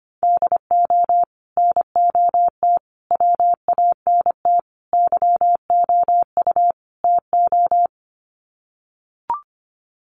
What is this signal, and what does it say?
Morse code, heard as a keyed tone, about 25 wpm, 700 Hz. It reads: DO NOT WANT YOU TO
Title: Sets of 5 Words 25wpm